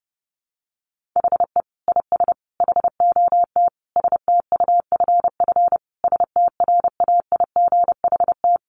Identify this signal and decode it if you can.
Morse code, heard as a keyed tone, about 30 wpm, 700 Hz. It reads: HE IS HOT STUFF STRAIGHT
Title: Sentences from Top 1500 Words 30wpm